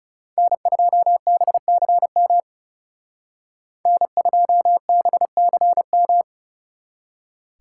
Morse code, heard as a keyed tone, about 36 wpm, 700 Hz. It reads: N2BCM N2BCM
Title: Speed Racing - USA 1x3 Call Signs 30wpm